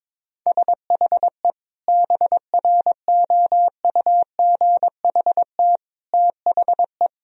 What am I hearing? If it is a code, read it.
Morse code, heard as a keyed tone, about 22 wpm, 700 Hz. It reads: SHE BROUGHT THE
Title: Sentences from Top 1000 Words - Review Part 4 - 22wpm